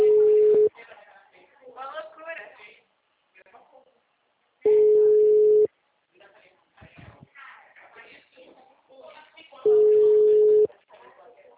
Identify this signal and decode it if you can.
Ringback tone (Europe, 425 Hz), with speech.
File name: out- -2005- - - .97532